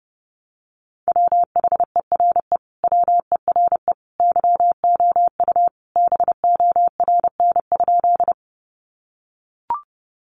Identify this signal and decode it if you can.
Morse code, heard as a keyed tone, about 30 wpm, 700 Hz. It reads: WHERE WERE YOU BORN?
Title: Sentences from Top 1000 Words 30wpm